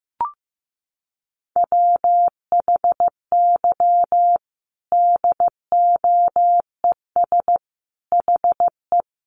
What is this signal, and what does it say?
Morse code, heard as a keyed tone, about 15 wpm, 700 Hz. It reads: WHY DOES HE
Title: Sentences from Top 1100 Words 15wpm